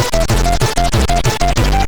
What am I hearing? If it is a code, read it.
There is no code.